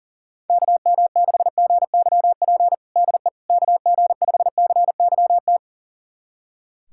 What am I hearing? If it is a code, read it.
Morse code, heard as a keyed tone, about 40 wpm, 700 Hz. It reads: KM6GYP DE KG5CYT
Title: USA Callsign DE Callsign - 6 Characters - 40wpm